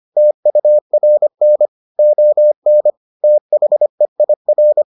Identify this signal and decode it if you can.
Morse code, heard as a keyed tone, about 25 wpm, 600 Hz. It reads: TURN ON THEIR